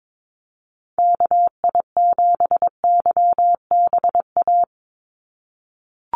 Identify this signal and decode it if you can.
Morse code, heard as a keyed tone, about 22 wpm, 700 Hz. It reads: KI7YBA
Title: USA Call Signs 22wpm